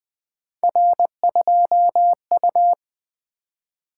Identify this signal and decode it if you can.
Morse code, heard as a keyed tone, about 20 wpm, 700 Hz. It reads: R2U
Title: Three Letter-Number Combinations - 20wpm